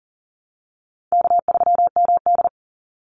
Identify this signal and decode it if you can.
Morse code, heard as a keyed tone, about 40 wpm, 700 Hz. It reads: K3MD
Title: Easy Call Signs Repeated 3x - 40wpm